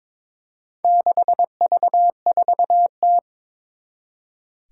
Morse code, heard as a keyed tone, about 22 wpm, 700 Hz. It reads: 6V4T
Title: Four Challenging Letter-Number Combinations 22wpm